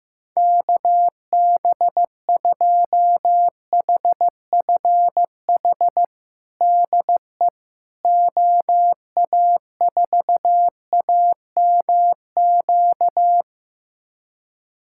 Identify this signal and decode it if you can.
Morse code, heard as a keyed tone, about 16 wpm, 700 Hz. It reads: KB2HFH DE OA4AMQ
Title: USA Callsign DE Callsign - 6 Characters - 15wpm